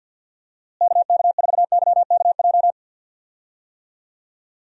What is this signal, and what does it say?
Morse code, heard as a keyed tone, about 45 wpm, 700 Hz. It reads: KK4YKJ
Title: USA Call Signs - 6 Characters - Repeated 2x - 50wpm